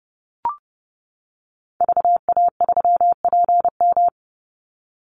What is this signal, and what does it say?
Morse code, heard as a keyed tone, about 30 wpm, 700 Hz. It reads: VA3PM
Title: Call Signs - 30wpm